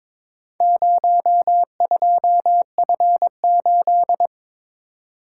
Morse code, heard as a keyed tone, about 22 wpm, 700 Hz. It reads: 02F8
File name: practice / 22wpm / Four Letter-Number Combinations - 22wpm